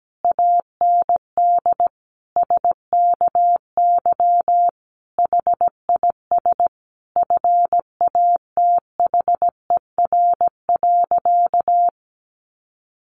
Morse code, heard as a keyed tone, about 17 wpm, 700 Hz. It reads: AND SKY HIS FATHER.
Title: Sentences from Top 1200 Words 17wpm